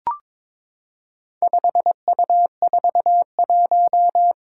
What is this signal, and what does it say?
Morse code, heard as a keyed tone, about 22 wpm, 700 Hz. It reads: 5U41
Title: Four Challenging Letter-Number Combinations 22wpm